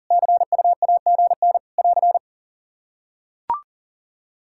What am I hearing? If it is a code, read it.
Morse code, heard as a keyed tone, about 37 wpm, 700 Hz. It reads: CUAGN <AR>
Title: Practice QSO - Exchange 4 - Ending - 40wpm